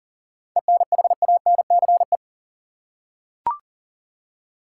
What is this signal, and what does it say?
Morse code, heard as a keyed tone, about 40 wpm, 700 Hz. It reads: ENHANCE
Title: Seven Letter Words - 40wpm